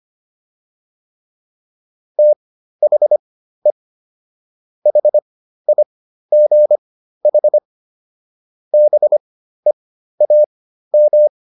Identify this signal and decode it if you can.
Morse code, heard as a keyed tone, about 25 wpm, spaced as for 13 wpm, 600 Hz. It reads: THE HIGH BEAM